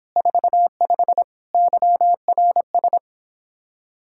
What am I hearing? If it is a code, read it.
Morse code, heard as a keyed tone, about 31 wpm, 700 Hz. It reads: 45 YRS